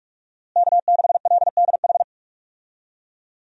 Morse code, heard as a keyed tone, about 46 wpm, 700 Hz. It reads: K6LDH
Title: USA Call Signs - 5 Characters - Repeated 2x - 45wpm